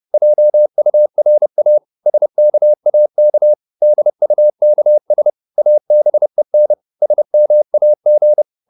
Morse code, heard as a keyed tone, about 30 wpm, 600 Hz. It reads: JURA SKAK DUKS ABEN SMAG
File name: Korte ord DK 30wpm